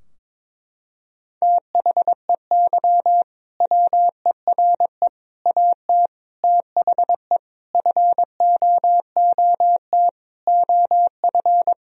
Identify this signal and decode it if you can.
Morse code, heard as a keyed tone, about 22 wpm, 700 Hz. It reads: THEY WERE AT THE FOOT OF
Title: Sentences from Top 1000 Words - Review Part 3 - 22wpm